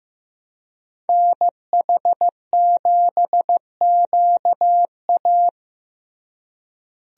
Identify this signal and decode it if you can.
Morse code, heard as a keyed tone, about 15 wpm, 700 Hz. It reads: NH7QA
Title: USA Call Signs - 5 Characters - Repeated 2x - 15wpm